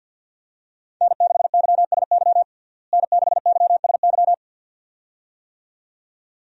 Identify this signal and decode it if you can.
Morse code, heard as a keyed tone, about 52 wpm, 700 Hz. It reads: N6YSY N6YSY
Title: USA Call Signs - 5 Characters - Repeated 2x - 50wpm